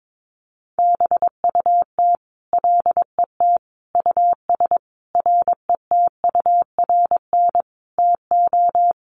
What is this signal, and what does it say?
Morse code, heard as a keyed tone, about 22 wpm, 700 Hz. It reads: BUT LET US RETURN TO
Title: Sentences from Top 800 Words 22wpm